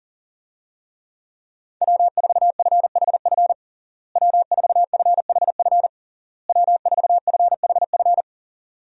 Morse code, heard as a keyed tone, about 40 wpm, 700 Hz. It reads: W4FHF W4FHF W4FHF
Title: USA Call Signs - 5 Characters - Repeated 3x - 40wpm